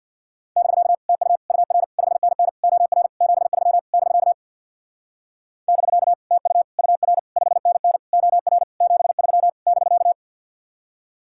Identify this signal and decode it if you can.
Morse code, heard as a keyed tone, about 60 wpm, 700 Hz. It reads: <BK> TU UR 5NN OR 73 <BK> <BK> TU UR 5NN OR 73 <BK>
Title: Speed-Racing Common QSO Phrases - 50wpm